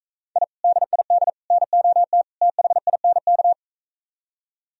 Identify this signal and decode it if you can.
Morse code, heard as a keyed tone, about 45 wpm, 700 Hz. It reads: I DID NOT THINK